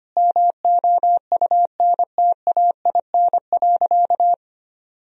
Morse code, heard as a keyed tone, about 25 wpm, 700 Hz. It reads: MOUNTAIN.
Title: Sentences from Top 1000 Words - Review Part 3 - 25wpm